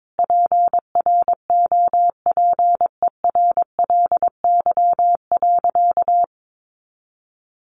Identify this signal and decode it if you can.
Morse code, heard as a keyed tone, about 22 wpm, 700 Hz. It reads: PROPERLY.
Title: Sentences from Top 1100 Words 22wpm